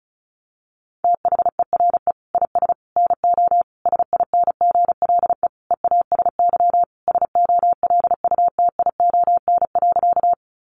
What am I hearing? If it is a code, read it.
Morse code, heard as a keyed tone, about 34 wpm, 700 Hz. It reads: THERE IS NO SINGLE EASY SOLUTION.
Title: Sentences from Top 1300 Words 35wpm